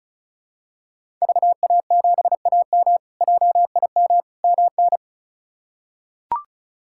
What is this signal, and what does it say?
Morse code, heard as a keyed tone, about 36 wpm, 700 Hz. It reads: VA7AM JIM MN
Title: K1USN SST QSO Practice - 35wpm